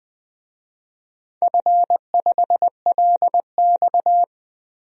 Morse code, heard as a keyed tone, about 20 wpm, 700 Hz. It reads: F5LX
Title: DXCC-100 Call Signs - 4 Characters - 20wpm